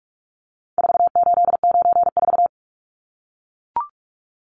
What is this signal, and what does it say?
Morse code, heard as a keyed tone, about 44 wpm, 700 Hz. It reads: 4894
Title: Numbers - 4-Digit 45wpm